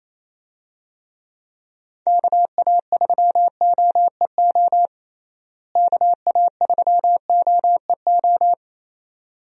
Morse code, heard as a keyed tone, about 27 wpm, 700 Hz. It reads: KA3OEO KA3OEO
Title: USA Call Signs - 6 Characters - Repeated 2x - 28wpm